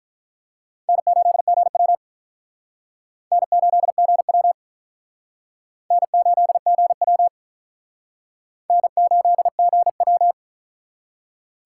Morse code, heard as a keyed tone, about 48 wpm, 700 Hz. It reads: N8GW N8GW N8GW N8GW